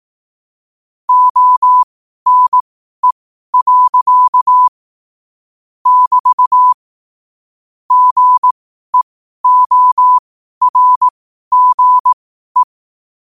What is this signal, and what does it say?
Morse code, heard as a keyed tone, about 18 wpm, spaced as for 12 wpm, 1000 Hz. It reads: ONE. <BT> GEORGE
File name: qotdfiles / 12wpm